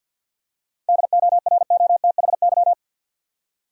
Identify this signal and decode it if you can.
Morse code, heard as a keyed tone, about 52 wpm, 700 Hz. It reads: DOROTHY
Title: USA Common Names - 50wpm